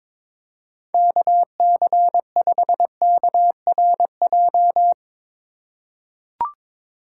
Morse code, heard as a keyed tone, about 22 wpm, 700 Hz. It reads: KC5KRJ
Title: USA Call Signs - 6 Characters - 3x Character Spacing - 22wpm